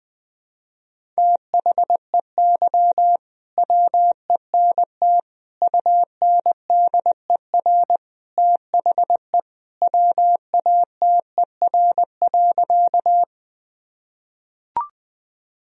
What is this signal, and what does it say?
Morse code, heard as a keyed tone, about 20 wpm, 700 Hz. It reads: THEY WENT UNDER THE WATER.
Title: Sentences from Top 1000 Words - Review Part 1 - 20wpm